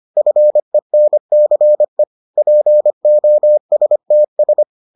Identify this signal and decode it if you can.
Morse code, heard as a keyed tone, about 25 wpm, 600 Hz. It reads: FENCE POSTS